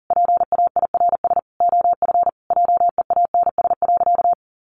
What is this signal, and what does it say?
Morse code, heard as a keyed tone, about 40 wpm, 700 Hz. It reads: PAIRS OF JEANS.
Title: Additional Easy Phrases 40wpm